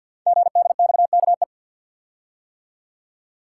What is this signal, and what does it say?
Morse code, heard as a keyed tone, about 48 wpm, 700 Hz. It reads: GDXKE
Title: Code Groups - 5 Letters - 50wpm